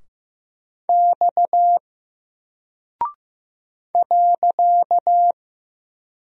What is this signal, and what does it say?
Morse code, heard as a keyed tone, about 15 wpm, 700 Hz. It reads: X .
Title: Single Character (Letter:Number:Punctuation:Prosign) Alt 15wpm